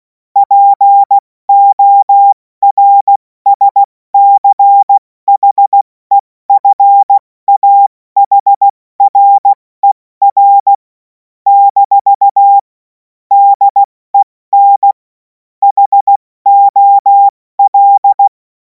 Morse code, heard as a keyed tone, about 16 wpm, 800 Hz. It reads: PORSCHEFAHRER - DEN HOL
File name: Morsekurs / Witze / Witz 05. Mai 2026 70 F80